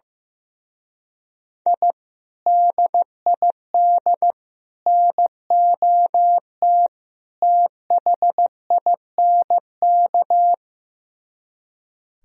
Morse code, heard as a keyed tone, about 15 wpm, 700 Hz. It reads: I DID NOT THINK
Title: Speed-Racing - Sets of 4 Words 15wpm